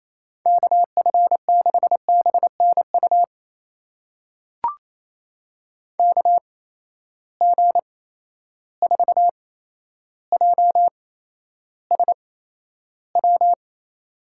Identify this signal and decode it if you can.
Morse code, heard as a keyed tone, about 28 wpm, spaced as for 12 wpm, 700 Hz. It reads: KF6BNU KG4JSW
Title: USA Call Signs - 6 Characters - 8x Character Spacing - 28wpm